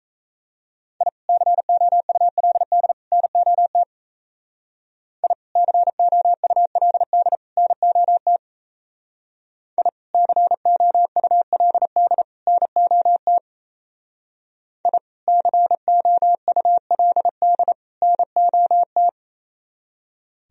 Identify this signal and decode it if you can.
Morse code, heard as a keyed tone, about 32 wpm, 700 Hz. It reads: I COULD NOT I COULD NOT I COULD NOT I COULD NOT
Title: Speed-Racing - Sets of 3 Words 28wpm